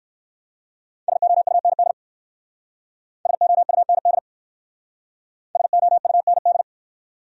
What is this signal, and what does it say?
Morse code, heard as a keyed tone, about 60 wpm, 700 Hz. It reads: SOUND SOUND SOUND